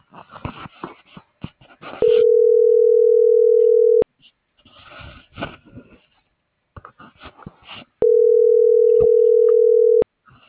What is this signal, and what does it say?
Ringback tone (North America), 440 + 480 Hz.